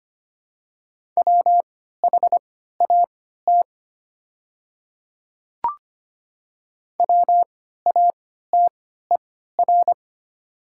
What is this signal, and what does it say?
Morse code, heard as a keyed tone, about 25 wpm, spaced as for 14 wpm, 690 Hz. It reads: WHAT WATER
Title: 3x Character Spacing - Top 200 Words - 25wpm